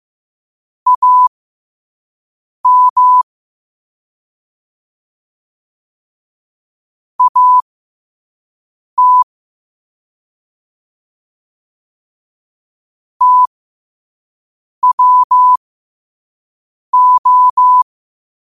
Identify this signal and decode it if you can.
Morse code, heard as a keyed tone, about 15 wpm, spaced as for 5 wpm, 1000 Hz. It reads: AM AT TWO
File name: qotdfiles / 5wpm